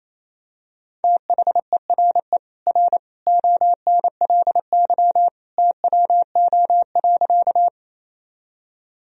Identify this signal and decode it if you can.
Morse code, heard as a keyed tone, about 28 wpm, 700 Hz. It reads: THERE R ONLY TWO.
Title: Sentences from Top 200 Words and Abbreviations 28wpm